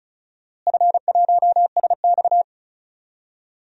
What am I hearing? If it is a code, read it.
Morse code, heard as a keyed tone, about 36 wpm, 700 Hz. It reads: F1SX
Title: DXCC-100 Call Signs - 4 Characters - Repeated 3x - 35wpm